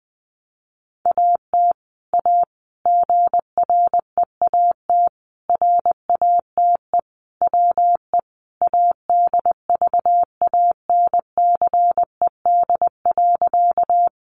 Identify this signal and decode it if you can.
Morse code, heard as a keyed tone, about 20 wpm, 700 Hz. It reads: AT A GREAT RATE WE ADVANCED.
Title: Sentences from Top 1200 Words 20wpm